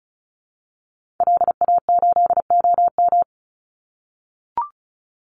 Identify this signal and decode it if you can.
Morse code, heard as a keyed tone, about 34 wpm, 700 Hz. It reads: LA8OM
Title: Call Signs - 35wpm